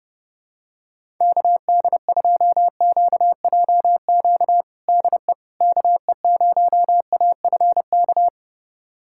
Morse code, heard as a keyed tone, about 32 wpm, 700 Hz. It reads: KD2QJQ DE KE0AFK
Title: USA Callsign DE Callsign 30wpm